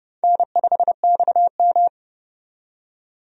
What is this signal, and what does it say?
Morse code, heard as a keyed tone, about 30 wpm, 700 Hz. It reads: N5XM
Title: USA Call Signs - 4 Characters - 30wpm